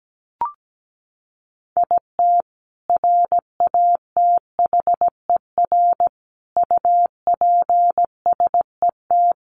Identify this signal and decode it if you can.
Morse code, heard as a keyed tone, about 17 wpm, 700 Hz. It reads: IT RATHER UPSET
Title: Sentences from Top 900 Words 17wpm